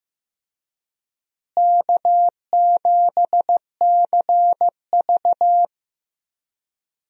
Morse code, heard as a keyed tone, about 16 wpm, 700 Hz. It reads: K7CV
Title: USA Call Signs - 4 Characters - Repeated 3x - 15wpm